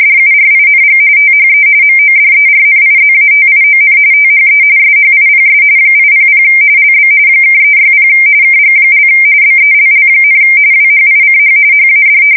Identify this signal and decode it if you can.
RTTY, about 45.45 baud. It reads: DURING THE MODERNIZATION THE SHIPS 800 LB (360 KG) BELL, WHICH HAD
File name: fsk170 / USS-Missouri